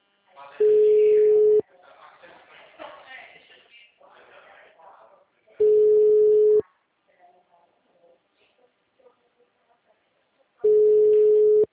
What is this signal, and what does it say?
Ringback tone (Europe, 425 Hz), with speech.
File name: out- -2011- - - .2342